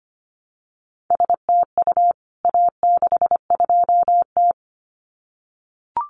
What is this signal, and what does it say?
Morse code, heard as a keyed tone, about 25 wpm, 700 Hz. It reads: STU A62T
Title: CWT Exchange 25wpm